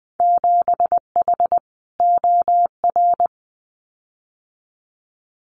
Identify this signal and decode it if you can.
Morse code, heard as a keyed tone, about 20 wpm, 700 Hz. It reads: 7H OR
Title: Winter Field Day Exchanges Encore 20wpm